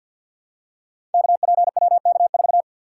Morse code, heard as a keyed tone, about 47 wpm, 700 Hz. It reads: KPWK4
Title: Code Groups - 5 Letter:Numbers - 50wpm